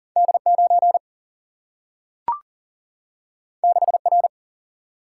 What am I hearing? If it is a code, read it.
Morse code, heard as a keyed tone, about 38 wpm, 700 Hz. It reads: D 9 6 R
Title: Single-Letter Single-Number 40wpm